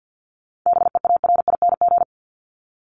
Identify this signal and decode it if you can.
Morse code, heard as a keyed tone, about 49 wpm, 700 Hz. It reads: BEARING
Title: ING Suffix - Top 100 Words with Base Words 50wpm